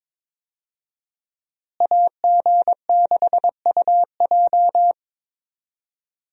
Morse code, heard as a keyed tone, about 22 wpm, 700 Hz. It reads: AG6UJ
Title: USA Call Signs - 5 Characters - Repeated 2x - 22wpm